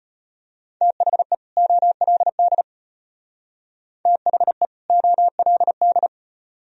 Morse code, heard as a keyed tone, about 31 wpm, 700 Hz. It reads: THE OLD THE OLD
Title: Speed-Racing - Sets of 2 Words 25wpm